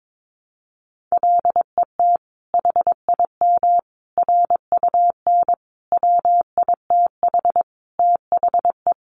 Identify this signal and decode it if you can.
Morse code, heard as a keyed tone, about 22 wpm, 700 Hz. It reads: LET HIM RUN WITH THE
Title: Sentences from Top 1300 Words 22wpm